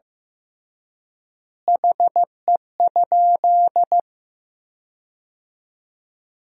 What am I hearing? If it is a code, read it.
Morse code, heard as a keyed tone, about 15 wpm, 700 Hz. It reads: HE?
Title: 3x Word-Spacing - Sentences from Top 100 Words - 15wpm